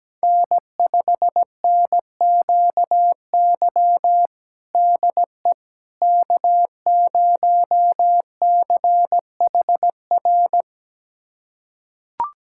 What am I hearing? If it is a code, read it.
Morse code, heard as a keyed tone, about 17 wpm, 700 Hz. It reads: N5NQY DE K0CHR
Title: USA Callsign DE Callsign - 5 Characters - 17wpm